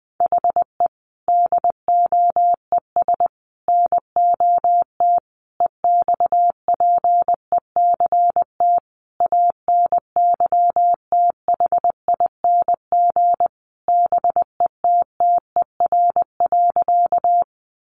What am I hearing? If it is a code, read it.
Morse code, heard as a keyed tone, about 20 wpm, 700 Hz. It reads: HE DOES NOT EXPECT ANYTHING BETTER.
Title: Sentences from Top 1100 Words 20wpm